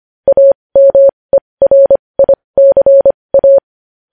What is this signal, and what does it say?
Morse code, heard as a keyed tone, about 25 wpm, spaced as for 20 wpm, 550 Hz. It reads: AMERICA
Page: News Headlines in Morse Code at 20 WPM